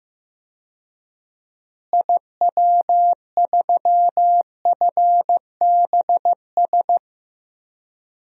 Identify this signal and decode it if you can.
Morse code, heard as a keyed tone, about 16 wpm, 700 Hz. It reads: IW3FBS
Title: DXCC-100 Call Signs - 6 Characters - Repeated 2x - 15wpm